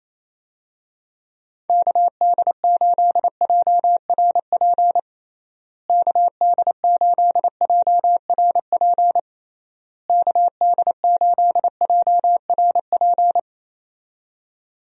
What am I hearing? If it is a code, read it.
Morse code, heard as a keyed tone, about 28 wpm, 700 Hz. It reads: KD8JRP KD8JRP KD8JRP
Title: USA Call Signs - 6 Characters - Repeated 3x - 28wpm